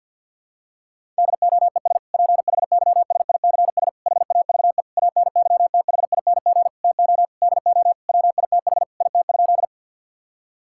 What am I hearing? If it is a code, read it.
Morse code, heard as a keyed tone, about 50 wpm, 700 Hz. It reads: DOES PHYSICS HAVE ANYTHING TO DO WITH IT?
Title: Sentences from Top 1000 Words - Review Part 5 - 50wpm